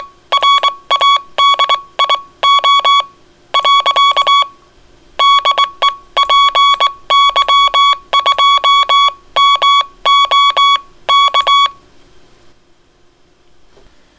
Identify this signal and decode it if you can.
Morse code, heard as a keyed tone, about 22 wpm, spaced as for 18 wpm, 1120 Hz. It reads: RADIO . DE PY2MOK